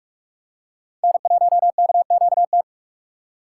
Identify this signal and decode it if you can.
Morse code, heard as a keyed tone, about 41 wpm, 690 Hz. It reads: N1KQT
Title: USA Call Signs - 4x Character Spacing - 45wpm